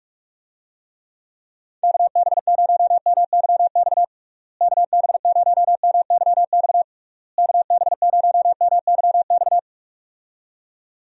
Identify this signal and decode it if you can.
Morse code, heard as a keyed tone, about 46 wpm, 700 Hz. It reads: KB0MYX KB0MYX KB0MYX
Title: USA Call Signs - 6 Characters - Repeated 3x - 45wpm